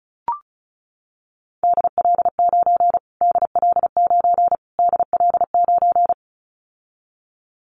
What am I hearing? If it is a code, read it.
Morse code, heard as a keyed tone, about 34 wpm, 700 Hz. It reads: DL9 DL9 DL9
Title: Callsign Prefixes for Top 25 DXCC - Repeated 3x - 35wpm